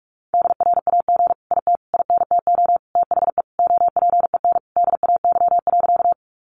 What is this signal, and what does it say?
Morse code, heard as a keyed tone, about 44 wpm, 700 Hz. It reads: DRAG IT INTO THE OPEN DAY.
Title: Sentences from Top 1500 Words 45wpm